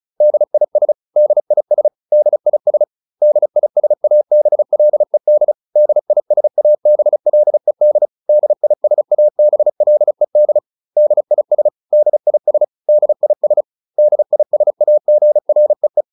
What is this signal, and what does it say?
Morse code, heard as a keyed tone, about 34 wpm, 600 Hz. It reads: DIS DIS DIS DISABLED DISABLED DISABLED DIS DIS DIS DISAGREE